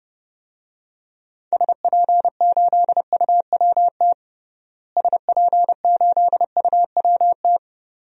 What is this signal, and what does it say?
Morse code, heard as a keyed tone, about 32 wpm, 700 Hz. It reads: SP8UWT SP8UWT
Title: DXCC-100 Call Signs - 6 Characters - Repeated 2x - 30wpm